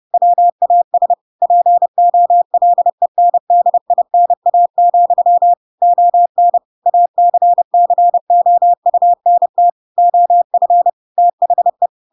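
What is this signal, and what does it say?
Morse code, heard as a keyed tone, about 30 wpm, 700 Hz. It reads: WAS POLENDINA, ON ACCOUNT OF THE
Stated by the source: Chapter 2 of The Adventures of Pinocchio read as a book in Morse code (CW) at 30 wpm, by ik1hge.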